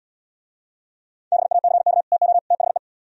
Morse code, heard as a keyed tone, about 72 wpm, 700 Hz. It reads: BACK TO THE